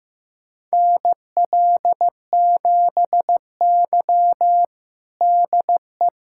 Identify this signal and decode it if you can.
Morse code, heard as a keyed tone, about 15 wpm, 700 Hz. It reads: NL7Y DE
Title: USA Callsign DE Callsign - 4 Characters - 15wpm